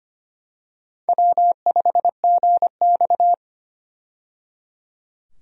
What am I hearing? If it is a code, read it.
Morse code, heard as a keyed tone, about 25 wpm, 700 Hz. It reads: W5GX
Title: USA Call Signs - 4 Characters - 25wpm